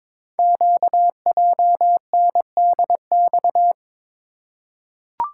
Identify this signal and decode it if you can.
Morse code, heard as a keyed tone, about 22 wpm, 700 Hz. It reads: QJNDX
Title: Code Groups - 5 Letters - 22wpm